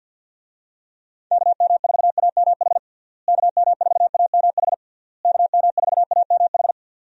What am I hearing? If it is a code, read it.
Morse code, heard as a keyed tone, about 51 wpm, 700 Hz. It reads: KM4AMH KM4AMH KM4AMH
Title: USA Call Signs - 6 Characters - Repeated 3x - 50wpm